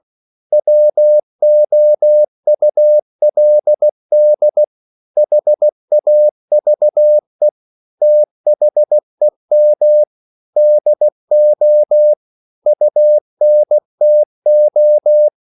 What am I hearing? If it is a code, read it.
Morse code, heard as a keyed tone, about 16 wpm, 600 Hz. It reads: WOULD HAVE THEM DO UNTO
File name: CW/Practice